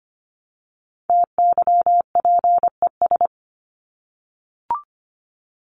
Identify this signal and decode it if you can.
Morse code, heard as a keyed tone, about 25 wpm, 700 Hz. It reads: TYPES
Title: ES:S Suffix - Top 100 Words with Base Words 25wpm